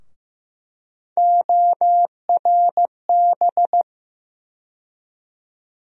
Morse code, heard as a keyed tone, about 15 wpm, 700 Hz. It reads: ORB
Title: USA Call Sign Postfixes - 3 Letters 15wpm